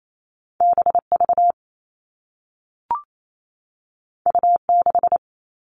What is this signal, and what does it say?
Morse code, heard as a keyed tone, about 28 wpm, 700 Hz. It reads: BV U6
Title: All Cut Numbers - 2-Digit 28wpm